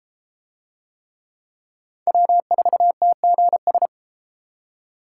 Morse code, heard as a keyed tone, about 32 wpm, 700 Hz. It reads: W4TGS